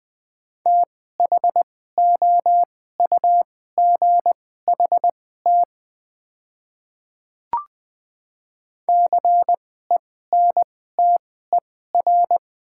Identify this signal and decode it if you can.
Morse code, heard as a keyed tone, about 20 wpm, spaced as for 14 wpm, 690 Hz. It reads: THOUGHT CENTER
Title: 2x Character Spacing - Top 500 Words - 20wpm